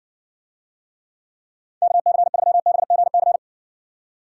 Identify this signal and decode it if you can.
Morse code, heard as a keyed tone, about 56 wpm, 700 Hz. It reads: KC3BGC